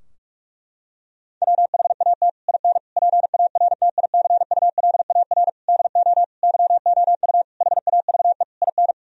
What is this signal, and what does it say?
Morse code, heard as a keyed tone, about 41 wpm, 700 Hz. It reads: WHAT IN PARTICULAR DO YOU HAVE IN
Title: Sentences from Top 1000 Words - Review Part 4 - 45wpm